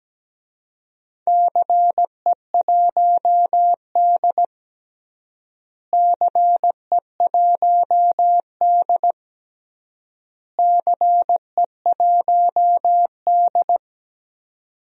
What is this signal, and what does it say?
Morse code, heard as a keyed tone, about 17 wpm, 700 Hz. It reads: CE1D CE1D CE1D
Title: DXCC-100 Call Signs - 4 Characters - Repeated 3x - 17wpm